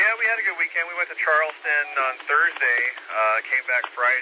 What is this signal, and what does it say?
Emergency Alert System (SAME) tones, heard as harsh buzzing bursts, with speech. It is heard from 0.9 to 3.9 s.